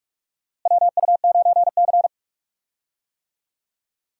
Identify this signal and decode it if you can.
Morse code, heard as a keyed tone, about 46 wpm, 700 Hz. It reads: WU9C
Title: USA Call Signs - 4 Characters - Repeated 2x - 45wpm